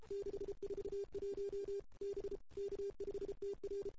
Morse code, heard as a keyed tone, about 30 wpm, 410 Hz. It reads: 641 B K5TR
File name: CW / 2008-ss-cw-k5tr-op-n5rz -03:00